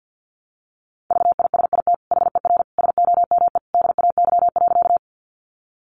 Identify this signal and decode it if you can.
Morse code, heard as a keyed tone, about 50 wpm, 700 Hz. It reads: VISIT HER SOME DAY.
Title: Sentences from Top 900 Words 50wpm